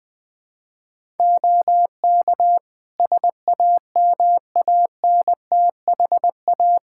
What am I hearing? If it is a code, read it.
Morse code, heard as a keyed tone, about 20 wpm, 700 Hz. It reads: OK SAMANTHA
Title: Practice QSO - Exchange 4 - Ending - 20wpm